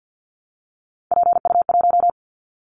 Morse code, heard as a keyed tone, about 50 wpm, 700 Hz. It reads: PU1
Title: Callsign Prefixes for Top 50 DXCC - 50wpm